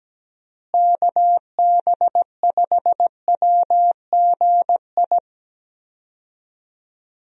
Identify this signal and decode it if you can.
Morse code, heard as a keyed tone, about 17 wpm, 700 Hz. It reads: KB5WGI
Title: USA Call Signs - 6 Characters - Repeated 3x - 17wpm